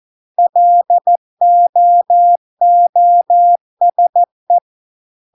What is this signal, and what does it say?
Morse code, heard as a keyed tone, about 14 wpm, 700 Hz. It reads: LOOSE